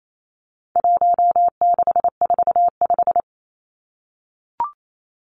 Morse code, heard as a keyed tone, about 28 wpm, 700 Hz. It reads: 1645
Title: Numbers - 4-Digit 28wpm